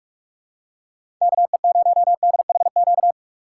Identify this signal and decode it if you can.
Morse code, heard as a keyed tone, about 45 wpm, 700 Hz. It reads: KE0DHQ
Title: USA Call Signs - 6 Characters - 4x Character Spacing - 45wpm